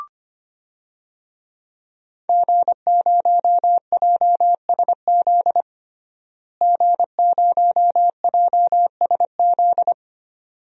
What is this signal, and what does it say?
Morse code, heard as a keyed tone, about 25 wpm, 700 Hz. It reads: G0JSZ G0JSZ
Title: DXCC-100 Call Signs - 5 Characters - Repeated 2x - 25wpm